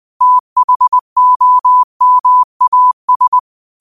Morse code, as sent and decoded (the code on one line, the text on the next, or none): - .... --- -- .- ...
THOMAS